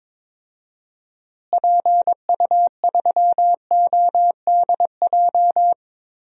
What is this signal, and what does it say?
Morse code, heard as a keyed tone, about 22 wpm, 700 Hz. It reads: PU3ODJ
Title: DXCC-100 Call Signs - 6 Characters - Repeated 3x - 22wpm